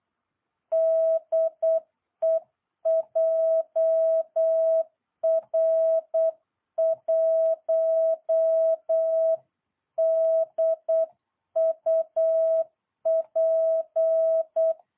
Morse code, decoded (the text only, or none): DE JR1DUP